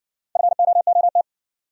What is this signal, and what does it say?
Morse code, heard as a keyed tone, about 60 wpm, 700 Hz. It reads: FOOT